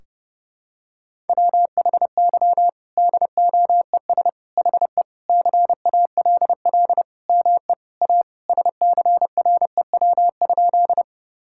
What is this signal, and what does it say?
Morse code, heard as a keyed tone, about 32 wpm, 700 Hz. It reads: WHY DOES HE CALL ME A SCREW?
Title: Sentences from Top 1600 Words 30wpm